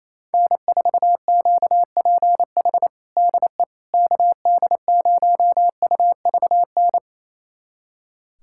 Morse code, heard as a keyed tone, about 28 wpm, 700 Hz. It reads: N4QPH DE KD0UVN
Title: USA Callsign DE Callsign 28wpm